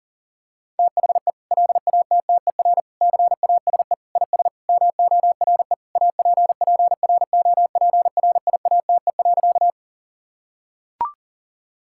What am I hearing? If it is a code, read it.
Morse code, heard as a keyed tone, about 40 wpm, 700 Hz. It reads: THE LATTER CASE IS MORE APPROPRIATE.
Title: Sentences from Top 1600 Words 40wpm